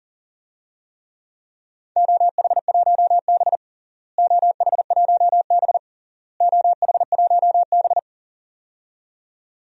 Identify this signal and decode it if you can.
Morse code, heard as a keyed tone, about 40 wpm, 700 Hz. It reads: OH1B OH1B OH1B
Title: DXCC-100 Call Signs - 4 Characters - Repeated 3x - 40wpm